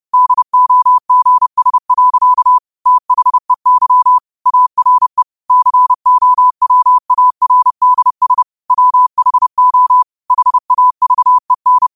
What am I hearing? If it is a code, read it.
Morse code, heard as a keyed tone, about 30 wpm, 1000 Hz. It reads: DOGS. THEY ARE COWARDS WHO HAVEN
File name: qotdfiles / 30wpm